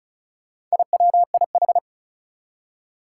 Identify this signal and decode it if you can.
Morse code, heard as a keyed tone, about 36 wpm, 700 Hz. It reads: IWIH
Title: Four Challenging Letter-Number Combinations 35wpm